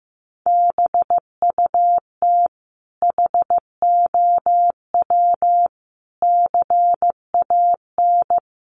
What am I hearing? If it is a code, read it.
Morse code, heard as a keyed tone, about 14 wpm, 700 Hz. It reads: BUT HOW CAN
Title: Sentences from Top 1400 Words 15wpm